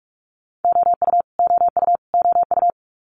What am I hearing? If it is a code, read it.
Morse code, heard as a keyed tone, about 46 wpm, 700 Hz. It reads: OU OU OU
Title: Multi-letter Phonemes Repeated 3x 45wpm